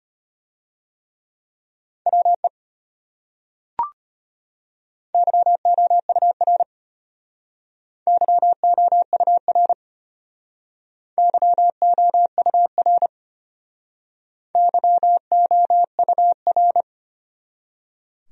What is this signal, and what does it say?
Morse code, heard as a keyed tone, about 33 wpm, 700 Hz. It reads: WE YOUR YOUR YOUR YOUR